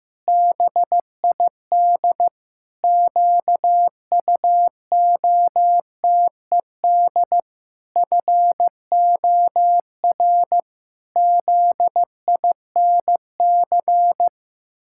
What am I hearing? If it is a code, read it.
Morse code, heard as a keyed tone, about 15 wpm, 700 Hz. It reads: BID QUOTED FOR ZINC
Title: Pangram Challenge v2 - 15wpm